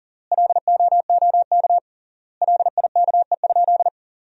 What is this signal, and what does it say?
Morse code, heard as a keyed tone, about 40 wpm, 700 Hz. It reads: LOOK LIKE?
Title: 3x Word-Spacing - Sentences from Top 100 Words - 40wpm